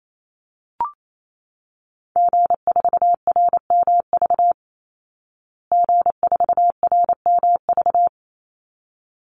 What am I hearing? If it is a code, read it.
Morse code, heard as a keyed tone, about 28 wpm, 700 Hz. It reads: G4RMV G4RMV
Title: Call Signs Repeated 3x - 28wpm